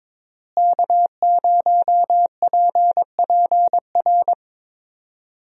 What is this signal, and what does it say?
Morse code, heard as a keyed tone, about 22 wpm, 700 Hz. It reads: K0PPR
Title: USA Call Signs - 5 Characters - Repeated 3x - 22wpm